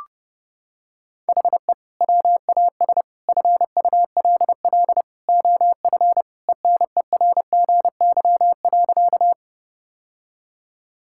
Morse code, heard as a keyed tone, about 30 wpm, 700 Hz. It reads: HE WAS FULL OF ENERGY.
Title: Sentences from Top 1000 Words - Review Part 6 - 30wpm